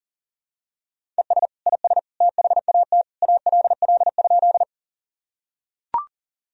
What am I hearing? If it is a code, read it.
Morse code, heard as a keyed tone, about 40 wpm, 700 Hz. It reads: ES IS THAT ALL?
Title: Sentences from Top 100 Words and Abbreviations 40wpm